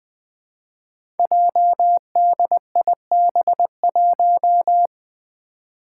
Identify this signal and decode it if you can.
Morse code, heard as a keyed tone, about 20 wpm, 700 Hz. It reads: JDIB1
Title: Five Challenging Letter-Number Combinations 20wpm